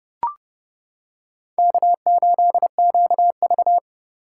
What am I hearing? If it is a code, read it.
Morse code, heard as a keyed tone, about 29 wpm, 700 Hz. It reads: K8QV